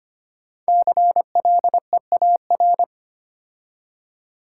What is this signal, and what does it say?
Morse code, heard as a keyed tone, about 25 wpm, 700 Hz. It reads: CLEAR
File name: 5 Letter Words - First 3 Letters Given - 25wpm